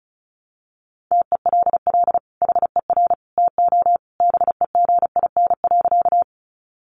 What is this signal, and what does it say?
Morse code, heard as a keyed tone, about 34 wpm, 700 Hz. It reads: TELL HER TO BEGIN.
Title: Sentences from Top 300 Words 35wpm